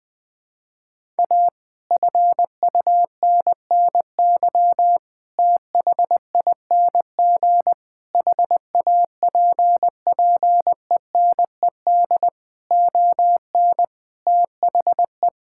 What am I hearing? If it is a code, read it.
Morse code, heard as a keyed tone, about 21 wpm, 700 Hz. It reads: A FUNNY THING HAPPENED ON THE
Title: Sets of 8 Words - 20wpm